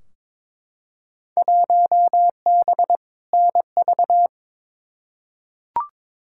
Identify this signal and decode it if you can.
Morse code, heard as a keyed tone, about 21 wpm, 700 Hz. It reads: 1B NV
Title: Field Day Exchanges - 22wpm